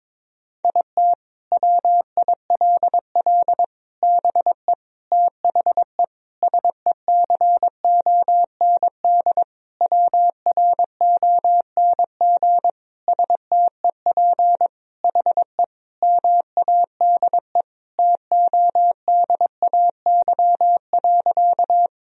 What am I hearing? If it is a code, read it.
Morse code, heard as a keyed tone, about 22 wpm, 700 Hz. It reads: IT WILL BE THE SECOND WRONG STEP HE MADE TODAY.
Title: Sentences from Top 1000 Words - Review Part 2 - 22wpm